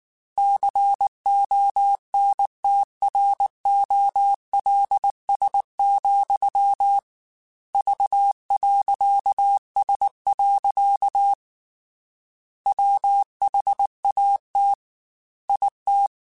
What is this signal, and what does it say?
Morse code, heard as a keyed tone, about 19 wpm, 800 Hz. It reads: CONTROLS, V.S. WHAT IT
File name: Technical Subjects / Technical Subjects-mp3s / OR Gate @18